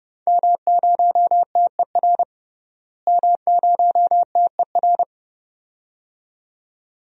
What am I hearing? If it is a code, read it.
Morse code, heard as a keyed tone, about 31 wpm, 700 Hz. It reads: M0TER M0TER
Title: DXCC-100 Call Signs - 5 Characters - Repeated 2x - 30wpm